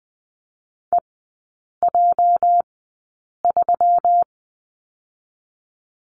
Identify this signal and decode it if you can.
Morse code, heard as a keyed tone, about 20 wpm, 700 Hz. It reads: E J 3
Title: 205 Lesson 33e - Letter j - Three Characters 20wpm